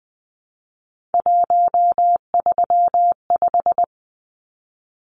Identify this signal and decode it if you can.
Morse code, heard as a keyed tone, about 20 wpm, 700 Hz. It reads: 135
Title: Numbers - 3-Digit 20wpm